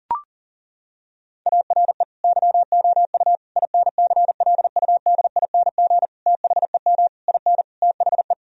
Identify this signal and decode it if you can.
Morse code, heard as a keyed tone, about 40 wpm, 700 Hz. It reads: ARE YOU INCLUDING THEM IN THE
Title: Sentences from Top 1000 Words - Review Part 4 - 40wpm